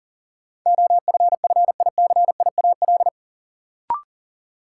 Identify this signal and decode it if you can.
Morse code, heard as a keyed tone, about 38 wpm, 700 Hz. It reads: OFFICIAL
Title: Sentences from Top 1000 Words - New Word Drill - 40wpm